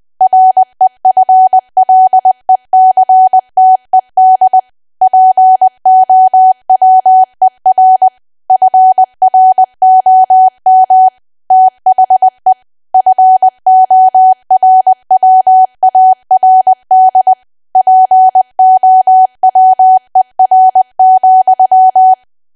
Morse code, heard as a keyed tone, about 20 wpm, 750 Hz. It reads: REFLECTED POWER FROM THE FORWARD POWER,
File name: cwa / W1AW-Level3 / 20WPM